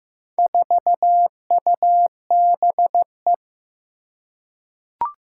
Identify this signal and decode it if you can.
Morse code, heard as a keyed tone, about 15 wpm, 700 Hz. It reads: 4UBE
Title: Four Challenging Letter-Number Combinations 15wpm